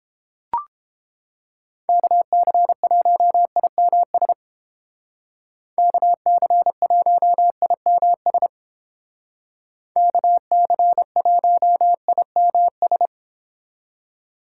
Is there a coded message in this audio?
Yes: Morse code from 1.9 to 13.1 s, heard as a keyed tone, about 31 wpm, 700 Hz. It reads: KC1IMS KC1IMS KC1IMS